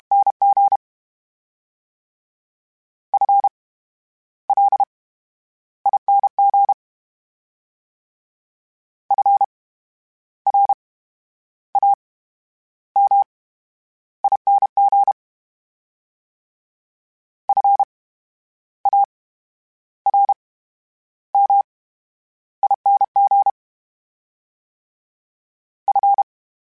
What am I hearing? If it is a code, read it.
Morse code, heard as a keyed tone, about 32 wpm, spaced as for 11 wpm, 800 Hz. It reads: NG FLING FRAMING FARMING F
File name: Special INGs / Special INGs-mp3s / VerySlowToFaster Ing @30